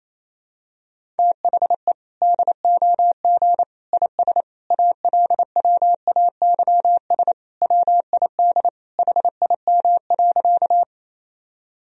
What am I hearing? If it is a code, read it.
Morse code, heard as a keyed tone, about 27 wpm, 700 Hz. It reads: THE DOG IS ALWAYS WID HIM.
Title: Sentences from Top 400 Words and Abbreviations 28wpm